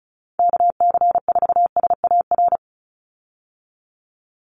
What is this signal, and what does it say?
Morse code, heard as a keyed tone, about 34 wpm, 700 Hz. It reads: KC4SAR
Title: USA Call Signs 35wpm